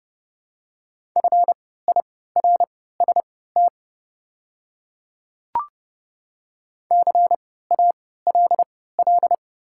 Morse code, heard as a keyed tone, about 30 wpm, spaced as for 17 wpm, 700 Hz. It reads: FIRST CALL
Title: 3x Character Spacing - Top 100 Words - 30wpm